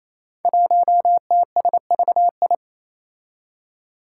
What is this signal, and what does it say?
Morse code, heard as a keyed tone, about 27 wpm, 700 Hz. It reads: 1TSVI
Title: Five Challenging Letter-Number Combinations 28wpm